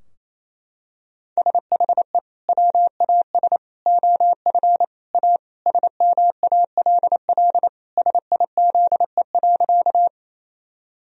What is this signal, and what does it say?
Morse code, heard as a keyed tone, about 28 wpm, 700 Hz. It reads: SHE WAS OF A SMALL SIZE.
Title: Sentences from Top 1000 Words - Review Part 2 - 28wpm